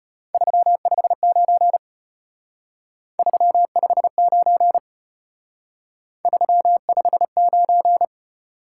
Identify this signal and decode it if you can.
Morse code, heard as a keyed tone, about 32 wpm, 700 Hz. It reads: 359 359 359